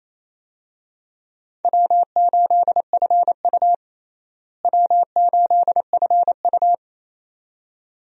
Morse code, heard as a keyed tone, about 28 wpm, 700 Hz. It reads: W8FU W8FU
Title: USA Call Signs - 4 Characters - Repeated 2x - 28wpm